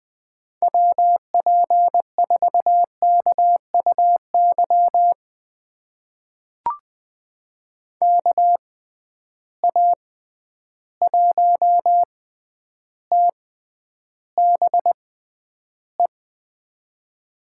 Morse code, heard as a keyed tone, about 20 wpm, spaced as for 10 wpm, 700 Hz. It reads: WP4KUY KA1TBE
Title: USA Call Signs - 6 Characters - 6x Character Spacing - 20wpm